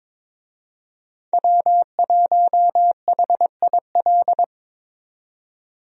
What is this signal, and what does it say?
Morse code, heard as a keyed tone, about 22 wpm, 700 Hz. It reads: W1HIL
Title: USA Call Signs - 5 Characters - 22wpm